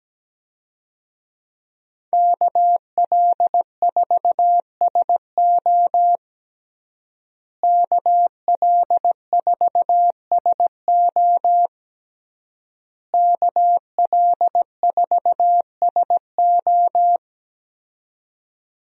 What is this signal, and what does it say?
Morse code, heard as a keyed tone, about 17 wpm, 700 Hz. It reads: KL4SO KL4SO KL4SO
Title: USA Call Signs - 5 Characters - Repeated 3x - 17wpm